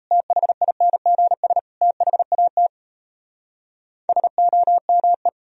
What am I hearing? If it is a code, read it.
Morse code, heard as a keyed tone, about 37 wpm, 700 Hz. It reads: THINGS THAT SOME